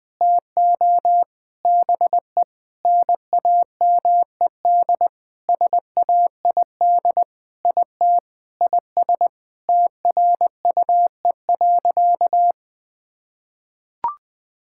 Morse code, heard as a keyed tone, about 21 wpm, 700 Hz. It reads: TO BE NAMED SAID IT IS TRUE.
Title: Sentences with 7-Word Phrases - 20wpm